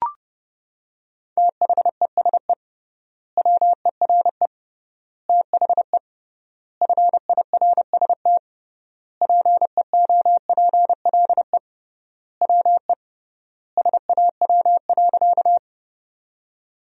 Morse code, heard as a keyed tone, about 29 wpm, 700 Hz. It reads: THESE WERE THE FIRST PEOPLE WE SAW.
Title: 3x Word-Spacing - Sentences from Top 300 Words - 30wpm